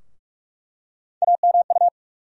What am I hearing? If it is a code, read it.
Morse code, heard as a keyed tone, about 43 wpm, 700 Hz. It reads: AMU
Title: USA Call Sign Postfixes - 3 Letters 45wpm